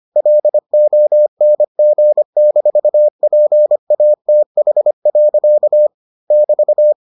Morse code, heard as a keyed tone, about 25 wpm, 600 Hz. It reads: LONG-PATH. <BT>